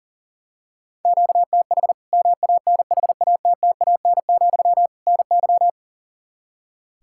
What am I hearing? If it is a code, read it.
Morse code, heard as a keyed tone, about 40 wpm, 700 Hz. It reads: QTH MANHATTAN, NY
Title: Practice QSO - Exchange 1 - Beginning - 40wpm